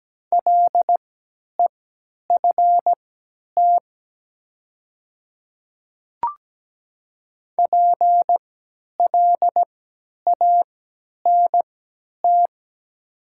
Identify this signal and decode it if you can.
Morse code, heard as a keyed tone, about 17 wpm, spaced as for 10 wpm, 700 Hz. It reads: LEFT PLANT
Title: 3x Character Spacing - Top 300 Words - 17wpm